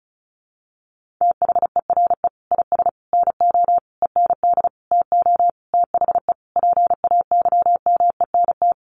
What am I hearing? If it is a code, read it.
Morse code, heard as a keyed tone, about 34 wpm, 700 Hz. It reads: THERE IS NO END TO THE PAYMENT
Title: Sentences from Top 1200 Words 35wpm